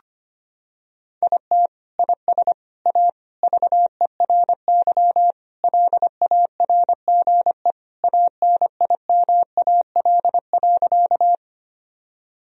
Morse code, heard as a keyed tone, about 25 wpm, 700 Hz. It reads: IT IS A VERY LARGE ANIMAL.
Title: Sentences from Top 1000 Words - Review Part 1 - 25wpm